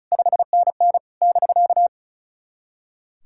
Morse code, heard as a keyed tone, about 36 wpm, 690 Hz. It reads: 5NN <BK>
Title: SOTA QSO Practice 35wpm